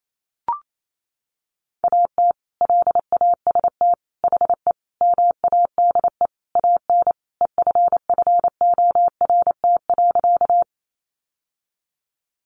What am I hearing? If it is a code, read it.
Morse code, heard as a keyed tone, about 28 wpm, 700 Hz. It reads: AT LAST HE MADE AN EFFORT.
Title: Sentences from Top 1200 Words 28wpm